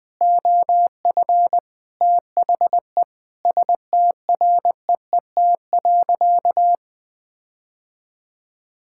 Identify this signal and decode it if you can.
Morse code, heard as a keyed tone, about 20 wpm, 700 Hz. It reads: OF THE STREET.
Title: Sentences with 5-Word Phrases - 20wpm